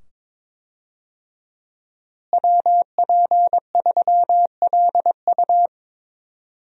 Morse code, heard as a keyed tone, about 22 wpm, 700 Hz. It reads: WP3LU
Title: USA Call Signs - 5 Characters - Repeated 2x - 22wpm